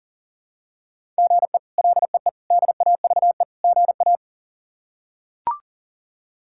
Morse code, heard as a keyed tone, about 40 wpm, 700 Hz. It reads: GE LEE DAVE GA
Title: K1USN SST QSO Practice - 40wpm